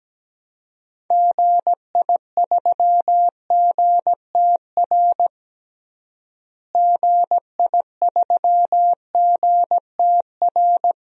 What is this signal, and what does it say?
Morse code, heard as a keyed tone, about 17 wpm, 700 Hz. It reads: GI3GTR GI3GTR
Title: DXCC-100 Call Signs - 6 Characters - Repeated 2x - 17wpm